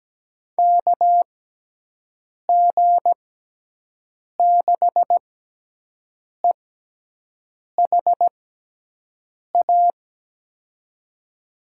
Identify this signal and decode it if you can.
Morse code, heard as a keyed tone, about 17 wpm, spaced as for 6 wpm, 700 Hz. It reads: KG6EHA
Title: USA Call Signs - 6 Characters - 6x Character Spacing - 17wpm